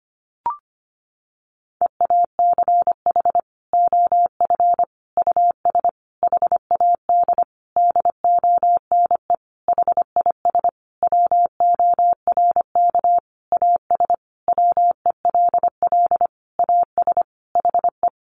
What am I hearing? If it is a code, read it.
Morse code, heard as a keyed tone, about 25 wpm, 700 Hz. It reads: EACH OF US HAD DONE HIS WORK AS WELL AS HE
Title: Sentences from Top 500 Words 25wpm